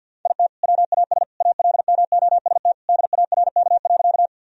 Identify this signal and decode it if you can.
Morse code, heard as a keyed tone, about 50 wpm, 700 Hz. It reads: IT WAS ALMOST DARK.
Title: Sentences from Top 1000 Words - Review Part 4 - 50wpm